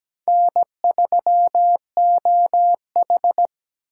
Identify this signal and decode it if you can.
Morse code, heard as a keyed tone, about 20 wpm, 700 Hz. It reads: N3OH
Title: Speed Racing - USA 1x2 Call Signs 17wpm